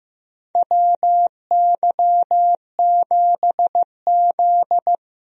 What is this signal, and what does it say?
Morse code, heard as a keyed tone, about 15 wpm, 700 Hz. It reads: WY7Z
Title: USA Call Signs - 4 Characters - Repeated 2x - 15wpm